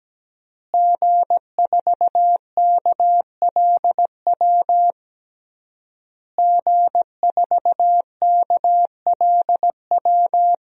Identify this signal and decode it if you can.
Morse code, heard as a keyed tone, about 17 wpm, 700 Hz. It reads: G4KLW G4KLW
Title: DXCC-100 Call Signs - 5 Characters - Repeated 3x - 17wpm